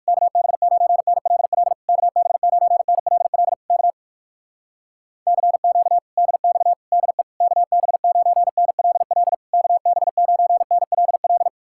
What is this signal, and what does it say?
Morse code, heard as a keyed tone, about 56 wpm, 700 Hz. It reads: KB9NLL KB9NLL K CQ DX DE KB9NLL KB9NLL
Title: Speed-Racing Common QSO Phrases Encore - 45wpm